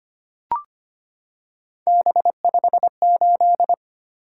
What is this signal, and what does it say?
Morse code, heard as a keyed tone, about 25 wpm, 700 Hz. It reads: B58
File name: Three Letter-Number Combinations - 25wpm